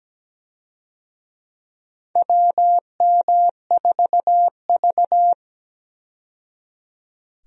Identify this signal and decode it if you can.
Morse code, heard as a keyed tone, about 17 wpm, 700 Hz. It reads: WM4V
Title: USA Call Signs - 4 Characters - 17wpm